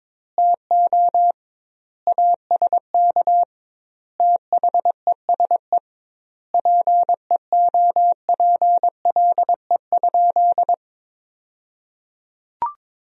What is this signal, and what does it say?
Morse code, heard as a keyed tone, about 21 wpm, 700 Hz. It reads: TO ASK THESE PEOPLE?
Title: 2x Word-Spacing - Sentences from Top 200 Words - 22wpm